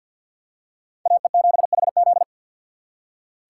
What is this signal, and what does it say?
Morse code, heard as a keyed tone, about 47 wpm, 690 Hz. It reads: AE7HZ
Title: USA Call Signs - 5 Characters - 6x Character Spacing - 50wpm